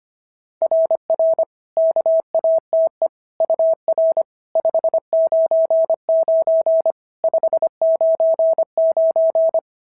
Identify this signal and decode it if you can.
Morse code, heard as a keyed tone, about 24 wpm, 650 Hz. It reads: RR KATE UR 599 599
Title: SOTA QSO Practice 25wpm